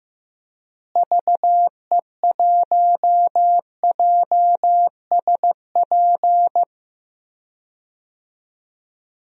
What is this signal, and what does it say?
Morse code, heard as a keyed tone, about 15 wpm, 700 Hz. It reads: VE1JSP
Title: DXCC-100 Call Signs - 6 Characters - Repeated 2x - 15wpm